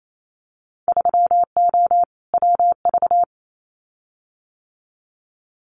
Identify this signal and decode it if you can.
Morse code, heard as a keyed tone, about 28 wpm, 700 Hz. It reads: 3O WV
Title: Winter Field Day Exchanges 28wpm